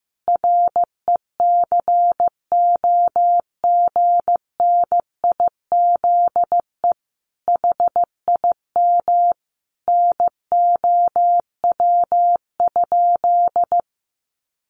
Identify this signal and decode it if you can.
Morse code, heard as a keyed tone, about 15 wpm, 700 Hz. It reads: RECOGNIZE HIM NOW?
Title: Sentences from Top 1400 Words 15wpm